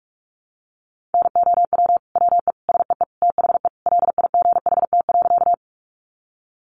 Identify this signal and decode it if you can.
Morse code, heard as a keyed tone, about 44 wpm, 700 Hz. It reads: NOW WE SEE THE LIGHT.
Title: Sentences from Top 200 Words 45wpm